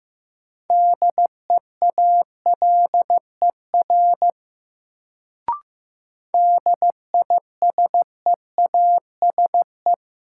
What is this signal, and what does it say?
Morse code, heard as a keyed tone, about 15 wpm, 700 Hz. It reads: DEALER DISEASE
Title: Sentences from Top 1100 Words - New Word Drill Alt - 15wpm